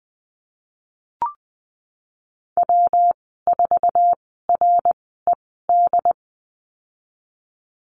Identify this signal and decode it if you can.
Morse code, heard as a keyed tone, about 20 wpm, spaced as for 15 wpm, 700 Hz. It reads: W4RED
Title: 186 Lesson 30g - Letter b - Call Signs 20wpm